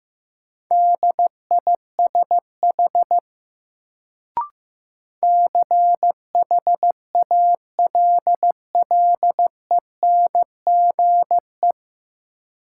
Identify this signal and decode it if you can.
Morse code, heard as a keyed tone, about 15 wpm, 700 Hz. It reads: DISH CHALLENGE
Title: Sentences from Top 1300 Words - New Word Drill Alt - 15wpm